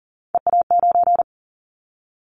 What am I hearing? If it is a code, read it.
Morse code, heard as a keyed tone, about 40 wpm, 700 Hz. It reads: EA9
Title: Callsign Prefixes for Top 100 DXCC - 40wpm